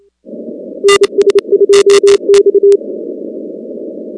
Morse code, heard as a keyed tone, about 28 wpm, 400 Hz. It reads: NS2X
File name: contests / 2019-cw-wpx-kt5j-k5pi- -2330 0-19